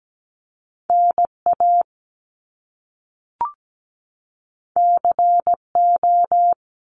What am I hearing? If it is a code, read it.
Morse code, heard as a keyed tone, about 17 wpm, 700 Hz. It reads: NA CO
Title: Top Two Letter Combinations in Words 17wpm